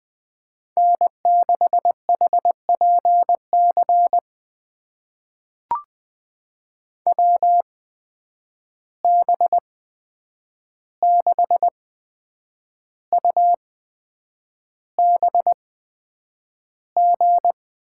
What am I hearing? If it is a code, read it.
Morse code, heard as a keyed tone, about 20 wpm, spaced as for 8 wpm, 700 Hz. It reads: N6HPC WB6UBG
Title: USA Call Signs - 8x Character Spacing - 20wpm